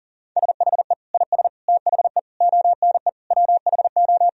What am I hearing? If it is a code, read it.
Morse code, heard as a keyed tone, about 37 wpm, 700 Hz. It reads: SHE IS THE ONE WHO
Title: Sentences from Top 1000 Words - Review Part 2 - 40wpm